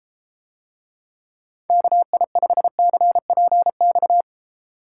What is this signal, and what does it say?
Morse code, heard as a keyed tone, about 30 wpm, 700 Hz. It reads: KI5CPX